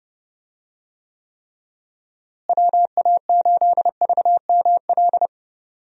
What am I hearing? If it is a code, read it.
Morse code, heard as a keyed tone, about 30 wpm, 700 Hz. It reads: WA8VML
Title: USA Call Signs - 6 Characters - Repeated 3x - 30wpm